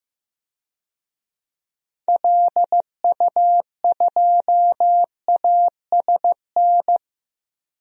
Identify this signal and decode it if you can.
Morse code, heard as a keyed tone, about 15 wpm, 700 Hz. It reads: LU2ASN
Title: DXCC-100 Call Signs - 6 Characters - 15wpm